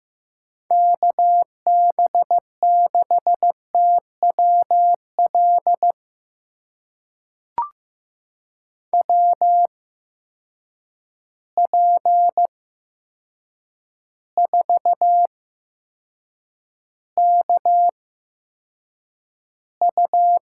Morse code, heard as a keyed tone, about 15 wpm, spaced as for 7 wpm, 700 Hz. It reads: KB6TWL WP4KU
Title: USA Call Signs - 6 Characters - 8x Character Spacing - 15wpm